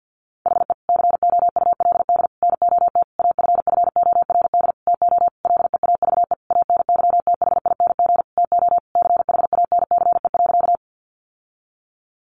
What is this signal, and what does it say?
Morse code, heard as a keyed tone, about 49 wpm, 700 Hz. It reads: HE COULD NOT AFFORD TO LEAVE ANYTHING TO CHANCE.
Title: Sentences from Top 1100 Words 50wpm